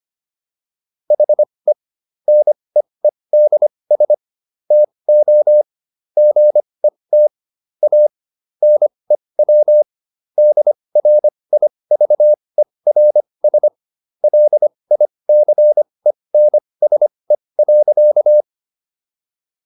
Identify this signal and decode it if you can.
Morse code, heard as a keyed tone, about 25 wpm, spaced as for 20 wpm, 600 Hz. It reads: HE NEEDS TO GET A NEW DRIVERS LICENSE.